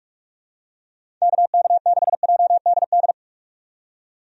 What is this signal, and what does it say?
Morse code, heard as a keyed tone, about 46 wpm, 700 Hz. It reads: KK6JDD
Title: USA Call Signs - 6 Characters - 45wpm